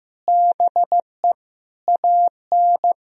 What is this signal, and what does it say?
Morse code, heard as a keyed tone, about 15 wpm, 700 Hz. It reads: BE AN
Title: Sentences from Top 1000 Words - Review Part 4 - 15wpm